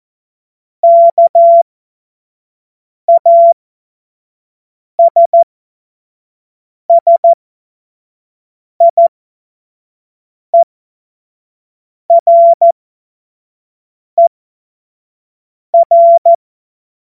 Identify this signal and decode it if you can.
Morse code, heard as a keyed tone, about 14 wpm, spaced as for 5 wpm, 680 Hz. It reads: KASSIERER
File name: Morsekurs / Witze / Witz 10. August 2021, 25 F70